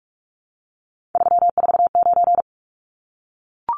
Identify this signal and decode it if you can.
Morse code, heard as a keyed tone, about 44 wpm, 700 Hz. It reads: 349
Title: Numbers - 3-Digit 45wpm